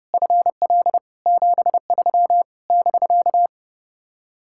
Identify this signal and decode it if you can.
Morse code, heard as a keyed tone, about 33 wpm, 700 Hz. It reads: FL 73 <BK>
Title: Speed-Racing Common QSO Phrases - 30wpm